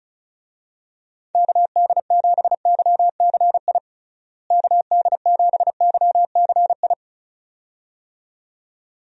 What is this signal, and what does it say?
Morse code, heard as a keyed tone, about 36 wpm, 700 Hz. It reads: KD7YCI KD7YCI
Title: USA Call Signs - 6 Characters - Repeated 2x - 35wpm